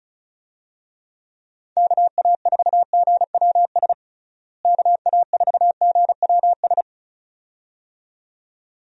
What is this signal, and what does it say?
Morse code, heard as a keyed tone, about 34 wpm, 700 Hz. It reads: KA4GWS KA4GWS
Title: USA Call Signs - 6 Characters - Repeated 2x - 35wpm